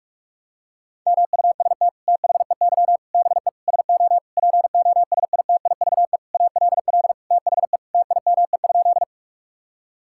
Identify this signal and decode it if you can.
Morse code, heard as a keyed tone, about 46 wpm, 700 Hz. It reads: MUST THEY BE SO POSITIVE ALL THE TIME?
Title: Sentences from Top 1000 Words - Review Part 5 - 45wpm